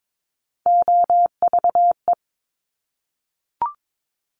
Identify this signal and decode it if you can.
Morse code, heard as a keyed tone, about 22 wpm, 690 Hz. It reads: OVE
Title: Top Three Letter Combinations in Words 22wpm